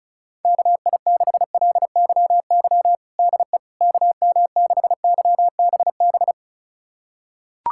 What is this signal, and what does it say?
Morse code, heard as a keyed tone, about 36 wpm, 700 Hz. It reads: KI6LYY DE KM6YBB
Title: USA Callsign DE Callsign - 6 Characters - 35wpm